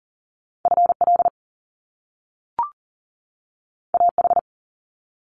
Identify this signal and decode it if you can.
Morse code, heard as a keyed tone, about 41 wpm, 700 Hz. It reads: FL AH
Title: Top Two Letter Combinations in Words 40wpm